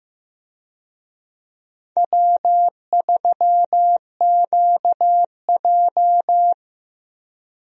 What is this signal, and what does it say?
Morse code, heard as a keyed tone, about 15 wpm, 700 Hz. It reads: W3QJ
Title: USA Call Signs - 4 Characters - Repeated 3x - 15wpm